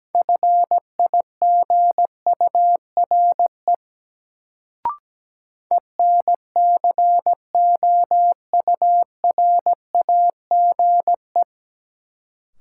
Morse code, heard as a keyed tone, about 17 wpm, 700 Hz. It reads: FIGURE ENCOURAGE
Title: Sentences from Top 1400 Words - New Word Drill Alt - 17wpm